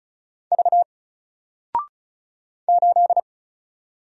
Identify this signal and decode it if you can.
Morse code, heard as a keyed tone, about 35 wpm, 700 Hz. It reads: V 8
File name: Single-Letter-Number-35wpm